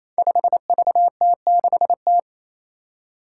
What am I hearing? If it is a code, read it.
Morse code, heard as a keyed tone, about 27 wpm, 700 Hz. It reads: 5VT6T
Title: Five Challenging Letter-Number Combinations 28wpm